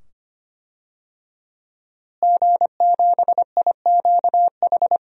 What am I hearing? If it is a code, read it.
Morse code, heard as a keyed tone, about 25 wpm, 700 Hz. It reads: G7IQH
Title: DXCC-100 Call Signs - 5 Characters - Repeated 2x - 25wpm